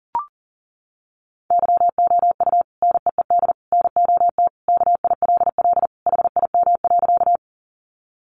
Morse code, heard as a keyed tone, about 40 wpm, 700 Hz. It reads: YOU NEED NOT KILL HIM.
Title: Sentences from Top 1400 Words 40wpm